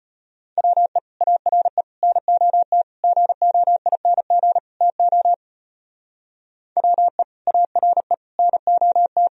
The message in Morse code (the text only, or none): WE ARE NOT GOING TO WE ARE NOT